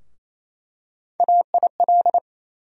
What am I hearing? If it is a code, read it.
Morse code, heard as a keyed tone, about 28 wpm, 700 Hz. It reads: AIL
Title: USA Call Sign Postfixes - 3 Letters 28wpm